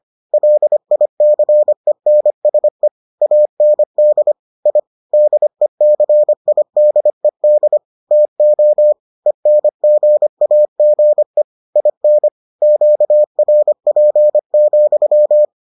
Morse code, heard as a keyed tone, about 25 wpm, 600 Hz. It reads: LICENSE AND I DECIDED TO ENGAGE IN QRP,